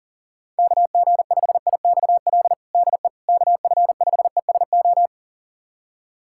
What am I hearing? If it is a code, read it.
Morse code, heard as a keyed tone, about 40 wpm, 700 Hz. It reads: KG5IXL DE KF5ESO
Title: USA Callsign DE Callsign - 6 Characters - 40wpm